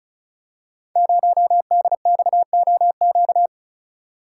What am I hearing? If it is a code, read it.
Morse code, heard as a keyed tone, about 36 wpm, 700 Hz. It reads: 0DXOQ
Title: Code Groups - 5 Letter:Numbers - 35wpm